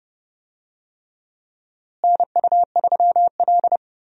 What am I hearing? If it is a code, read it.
Morse code, heard as a keyed tone, about 31 wpm, 700 Hz. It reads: NU3L
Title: USA Call Signs - 4 Characters - Repeated 3x - 30wpm